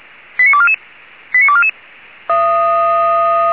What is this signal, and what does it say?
Five-tone selective call (ZVEI), heard as quick runs of five tones, tones of about 70 ms. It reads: 89244 89244